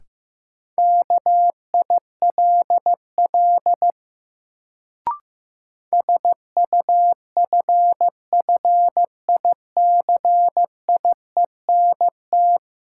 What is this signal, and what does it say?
Morse code, heard as a keyed tone, about 15 wpm, 700 Hz. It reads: KILL SUFFICIENT
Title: Sentences from Top 1400 Words - New Word Drill - 15wpm